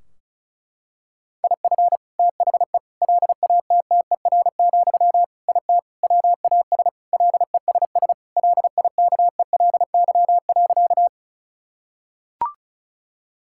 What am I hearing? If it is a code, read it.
Morse code, heard as a keyed tone, about 34 wpm, 700 Hz. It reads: IF THE LATTER, IT WAS LESS LIKELY.
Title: Sentences from Top 1600 Words 35wpm